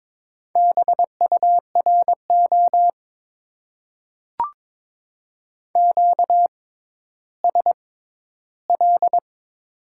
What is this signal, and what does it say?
Morse code, heard as a keyed tone, about 22 wpm, spaced as for 13 wpm, 700 Hz. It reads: BURO QSL
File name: Top 90 QSO Elements - 6x Character Spacing - 22wpm